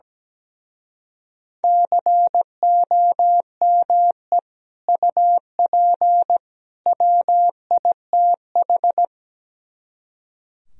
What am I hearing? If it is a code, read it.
Morse code, heard as a keyed tone, about 17 wpm, 700 Hz. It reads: COME UP WITH
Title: Speed-Racing - Sets of 3 Words 17wpm